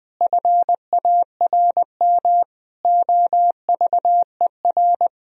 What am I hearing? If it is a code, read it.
Morse code, heard as a keyed tone, about 20 wpm, 700 Hz. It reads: FARM OVER
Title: Sentences from Top 1000 Words - Review Part 2 - 20wpm